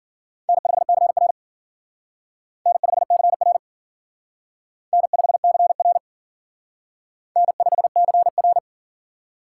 Morse code, heard as a keyed tone, about 52 wpm, 700 Hz. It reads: N5CR N5CR N5CR N5CR